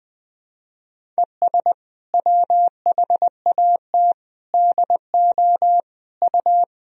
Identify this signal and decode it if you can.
Morse code, heard as a keyed tone, about 20 wpm, 700 Hz. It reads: ES WHAT DO U
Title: Sentences from Top 200 Words and Abbreviations 20wpm